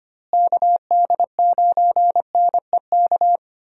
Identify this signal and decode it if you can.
Morse code, heard as a keyed tone, about 25 wpm, 700 Hz. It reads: KD9NEK
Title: USA Call Signs - 6 Characters - Repeated 3x - 25wpm